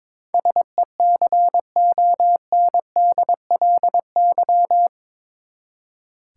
Morse code, heard as a keyed tone, about 22 wpm, 700 Hz. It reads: SECONDLY
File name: practice / 22wpm / Eight Letter Words - 22wpm